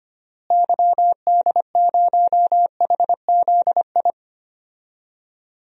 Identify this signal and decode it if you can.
Morse code, heard as a keyed tone, about 25 wpm, 700 Hz. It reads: YD0HZI
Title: DXCC-100 Call Signs - 6 Characters - Repeated 3x - 25wpm